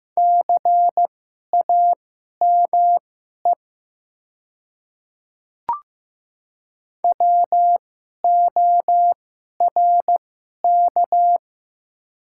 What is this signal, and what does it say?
Morse code, heard as a keyed tone, about 15 wpm, spaced as for 11 wpm, 700 Hz. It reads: CAME WORK
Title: 2x Character Spacing - Top 300 Words - 15wpm